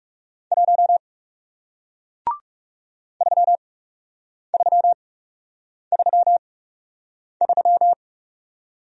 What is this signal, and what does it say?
Morse code, heard as a keyed tone, about 40 wpm, spaced as for 8 wpm, 700 Hz. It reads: 1 3333